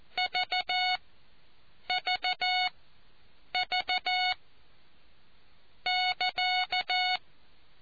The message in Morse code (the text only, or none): VVV<KA>